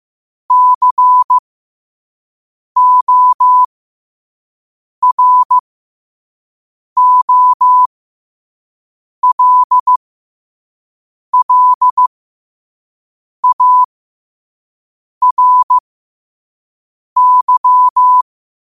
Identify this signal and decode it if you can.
Morse code, heard as a keyed tone, about 15 wpm, spaced as for 5 wpm, 1000 Hz. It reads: COROLLARY
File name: qotdfiles / 5wpm